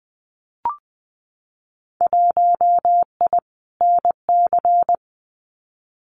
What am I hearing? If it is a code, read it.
Morse code, heard as a keyed tone, about 20 wpm, 700 Hz. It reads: 1I NC
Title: Winter Field Day Exchanges Encore 20wpm